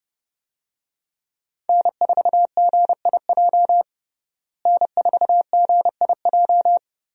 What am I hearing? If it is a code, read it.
Morse code, heard as a keyed tone, about 30 wpm, 700 Hz. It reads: N4GIJ N4GIJ
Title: USA Call Signs - 5 Characters - Repeated 2x - 30wpm